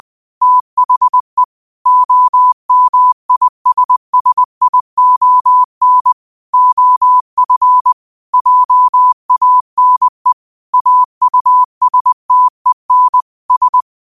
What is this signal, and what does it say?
Morse code, heard as a keyed tone, about 20 wpm, 1000 Hz. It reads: THE OMISSION OF JANE AUSTEN S
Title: 20 WPM morse code quotes for Thu, 29 Feb 2024